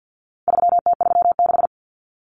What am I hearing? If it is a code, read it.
Morse code, heard as a keyed tone, about 50 wpm, 700 Hz. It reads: 3T36
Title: Common Cut Numbers - 4-Digit 50wpm